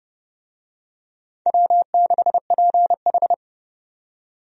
Morse code, heard as a keyed tone, about 30 wpm, 700 Hz. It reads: W6PH
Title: USA Call Signs - 4 Characters - 30wpm